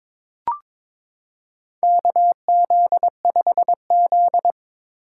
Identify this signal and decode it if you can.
Morse code, heard as a keyed tone, about 22 wpm, 700 Hz. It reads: KZ5Z
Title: K1USN SST QSO Practice - 22wpm